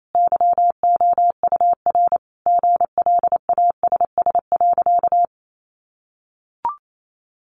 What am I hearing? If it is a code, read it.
Morse code, heard as a keyed tone, about 28 wpm, 700 Hz. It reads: YOUR GLASS.
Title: Sentences from Top 1300 Words 28wpm